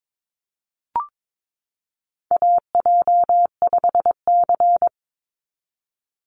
Morse code, heard as a keyed tone, about 22 wpm, 700 Hz. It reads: AJ5C
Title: NA SOTA Chaser Callsigns 22wpm